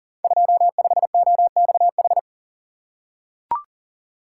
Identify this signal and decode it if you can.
Morse code, heard as a keyed tone, about 38 wpm, 700 Hz. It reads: 25OXH
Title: Code Groups - 5 Letter:Numbers - 40wpm